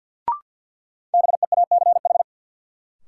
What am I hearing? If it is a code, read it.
Morse code, heard as a keyed tone, about 48 wpm, 700 Hz. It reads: BEACH
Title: Sentences from Top 1500 Words - New Word Drill Alt - 50wpm